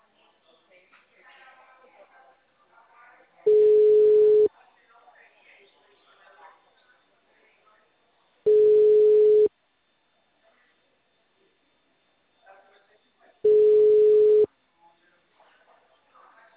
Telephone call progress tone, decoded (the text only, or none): ringback tone